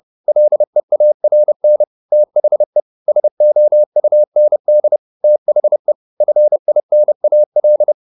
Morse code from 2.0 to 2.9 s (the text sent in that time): THE